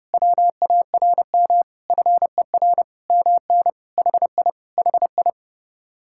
Morse code, heard as a keyed tone, about 30 wpm, 700 Hz. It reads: WARM FER MN HI HI